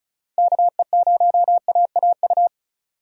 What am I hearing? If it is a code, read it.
Morse code, heard as a keyed tone, about 34 wpm, 700 Hz. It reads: KE0AAU
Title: USA Call Signs - 6 Characters - 4x Character Spacing - 35wpm